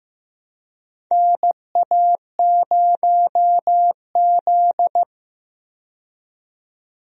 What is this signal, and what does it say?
Morse code, heard as a keyed tone, about 15 wpm, 700 Hz. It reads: NA0Z
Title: USA Call Signs - 4 Characters - 15wpm